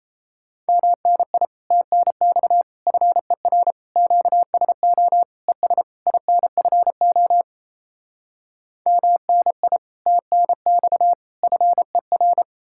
Morse code, heard as a keyed tone, about 37 wpm, 700 Hz. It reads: MNI TNX FER QSO ES INFO MNI TNX FER